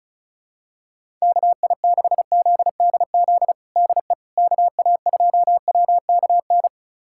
Morse code, heard as a keyed tone, about 36 wpm, 690 Hz. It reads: KI6ZDZ DE KA2WKN
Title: USA Callsign DE Callsign 35wpm